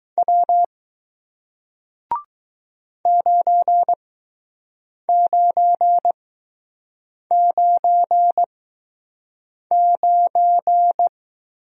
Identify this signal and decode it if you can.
Morse code, heard as a keyed tone, about 20 wpm, spaced as for 6 wpm, 700 Hz. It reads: W 9999